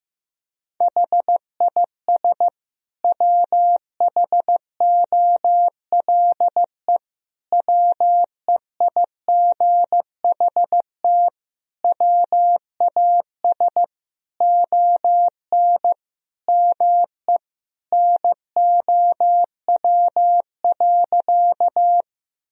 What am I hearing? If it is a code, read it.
Morse code, heard as a keyed tone, about 15 wpm, 700 Hz. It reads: HIS WHOLE WEIGHT WAS ON ME NOW.
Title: Sentences from Top 1000 Words - Review Part 3 - 15wpm